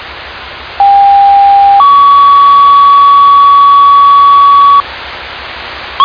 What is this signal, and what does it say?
Two-tone page, heard as a one-second tone, then a three-second tone, about 800 then 1130 Hz. It is heard from 0.8 to 4.8 s.